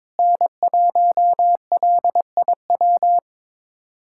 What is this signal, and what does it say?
Morse code, heard as a keyed tone, about 22 wpm, 700 Hz. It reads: N1LIW
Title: USA Callsign DE Callsign - 5 Characters - 22wpm